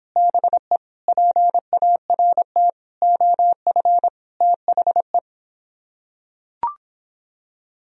Morse code, heard as a keyed tone, about 26 wpm, 700 Hz. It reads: BE PART OF THE